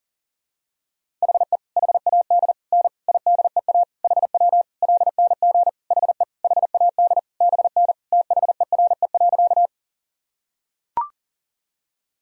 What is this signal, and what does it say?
Morse code, heard as a keyed tone, about 40 wpm, 700 Hz. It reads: HE HAD N IDEA HW LNG HE HAD BN THERE.
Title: Sentences from Top 300 Words and Abbreviations 40wpm